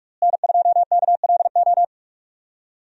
Morse code, heard as a keyed tone, about 46 wpm, 700 Hz. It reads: N2KLO
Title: USA Call Signs - 5 Characters - Repeated 3x - 45wpm